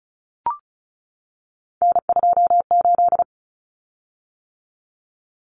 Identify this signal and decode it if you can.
Morse code, heard as a keyed tone, about 34 wpm, 700 Hz. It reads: N28
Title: Common Cut Numbers - 3-Digit 35wpm